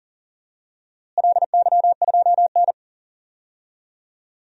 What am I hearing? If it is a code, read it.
Morse code, heard as a keyed tone, about 40 wpm, 700 Hz. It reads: LY2N
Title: DXCC-100 Call Signs - 4 Characters - Repeated 2x - 40wpm